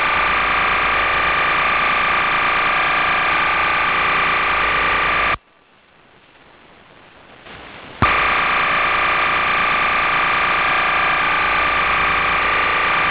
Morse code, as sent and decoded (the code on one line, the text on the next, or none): none